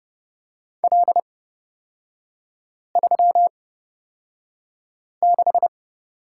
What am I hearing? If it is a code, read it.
Morse code, heard as a keyed tone, about 29 wpm, spaced as for 5 wpm, 700 Hz. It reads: L36
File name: Lesson 38e - Letter q - Three Characters 30wpm